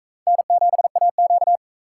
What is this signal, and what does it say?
Morse code, heard as a keyed tone, about 44 wpm, 700 Hz. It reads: N7AQ